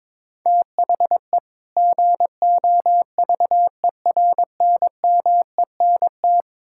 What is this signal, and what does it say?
Morse code, heard as a keyed tone, about 21 wpm, 700 Hz. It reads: THE GOVERNMENT
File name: Build A Phrase - 2-Word Phrases - 22wpm